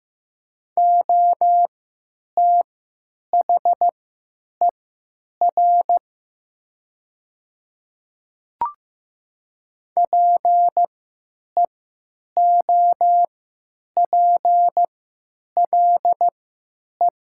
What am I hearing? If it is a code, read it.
Morse code, heard as a keyed tone, about 15 wpm, spaced as for 9 wpm, 700 Hz. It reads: OTHER PEOPLE
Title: 3x Character Spacing - Top 100 Words - 15wpm